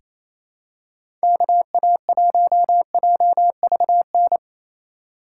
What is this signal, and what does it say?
Morse code, heard as a keyed tone, about 28 wpm, 700 Hz. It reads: KA1JVN
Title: USA Call Signs - 6 Characters - 2x Character Spacing - 28wpm